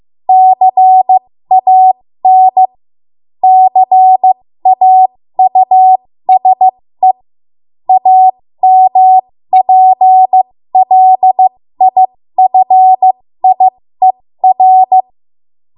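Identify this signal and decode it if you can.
Morse code, heard as a keyed tone, about 15 wpm, spaced as for 13 wpm, 750 Hz. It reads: CAN CAUSE AMPLIFIER
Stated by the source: 13 wpm Morse code practice archive transmission.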